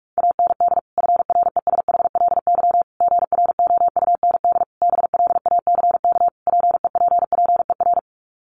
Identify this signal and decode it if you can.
Morse code, heard as a keyed tone, about 44 wpm, 700 Hz. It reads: AND FRESHLY GROUND BLACK PEPPER
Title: Sets of 5 Words Encore 45wpm